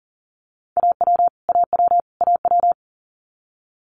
Morse code, heard as a keyed tone, about 38 wpm, 700 Hz. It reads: AW AW AW
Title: Multi-letter Phonemes Repeated 3x 40wpm